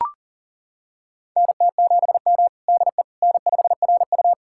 Morse code, heard as a keyed tone, about 40 wpm, 700 Hz. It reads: NT7M DE N5RU
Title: USA Callsign DE Callsign - 4 Characters - 40wpm